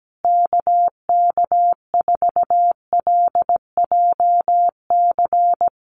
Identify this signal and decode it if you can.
Morse code, heard as a keyed tone, about 17 wpm, 700 Hz. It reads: KK4LJC
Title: USA Call Signs Repeated 3x 17wpm